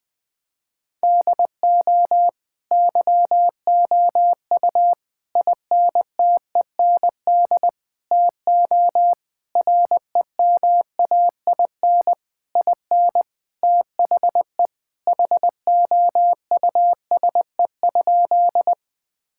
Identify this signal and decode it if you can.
Morse code, heard as a keyed tone, about 20 wpm, 700 Hz. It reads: DO YOU INTEND TO REMAIN IN THE HOUSE?
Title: Sentences from Top 1600 Words 20wpm